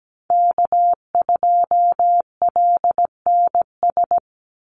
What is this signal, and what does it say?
Morse code, heard as a keyed tone, about 17 wpm, 700 Hz. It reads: K2LNS
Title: Call Signs - 17wpm